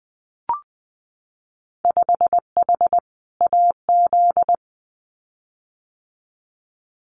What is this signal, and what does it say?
Morse code, heard as a keyed tone, about 20 wpm, 700 Hz. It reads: 5H AZ
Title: Winter Field Day Exchanges Encore 20wpm